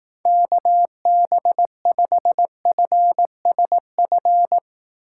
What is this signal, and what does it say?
Morse code, heard as a keyed tone, about 18 wpm, 700 Hz. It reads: KB5FSF
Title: Speed Racing - USA 2x3 Call Signs 15wpm